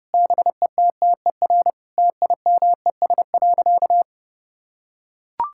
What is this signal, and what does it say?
Morse code, heard as a keyed tone, about 31 wpm, 700 Hz. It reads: BETTER TIMES.
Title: Sentences from Top 1000 Words - Review Part 6 - 30wpm